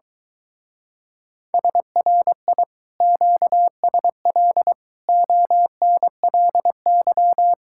Morse code, heard as a keyed tone, about 23 wpm, 700 Hz. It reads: SRI QSL ONLY